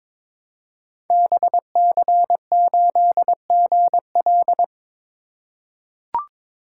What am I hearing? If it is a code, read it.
Morse code, heard as a keyed tone, about 22 wpm, 700 Hz. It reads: BC8GL
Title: Code Groups - 5 Letter:Numbers - 22wpm